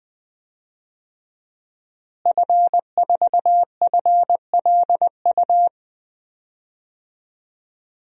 Morse code, heard as a keyed tone, about 21 wpm, 700 Hz. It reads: F4FLU
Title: DXCC-100 Call Signs - 5 Characters - Repeated 3x - 20wpm